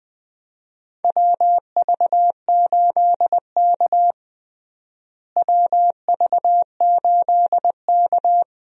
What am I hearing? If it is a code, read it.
Morse code, heard as a keyed tone, about 20 wpm, 700 Hz. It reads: WV8K WV8K
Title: USA Call Signs - 4 Characters - Repeated 3x - 20wpm